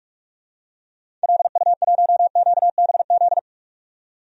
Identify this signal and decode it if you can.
Morse code, heard as a keyed tone, about 43 wpm, 700 Hz. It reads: LU1QBZ
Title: DXCC-100 Call Signs - 6 Characters - 45wpm